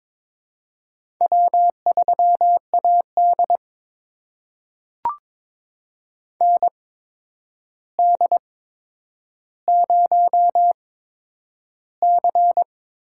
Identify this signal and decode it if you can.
Morse code, heard as a keyed tone, about 22 wpm, spaced as for 9 wpm, 700 Hz. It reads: W3AD ND0C
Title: USA Call Signs - 4 Characters - 8x Character Spacing - 22wpm